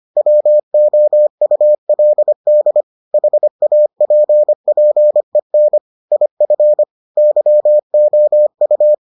Morse code, heard as a keyed tone, about 25 wpm, 600 Hz. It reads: WOULD HAPPEN IF YOU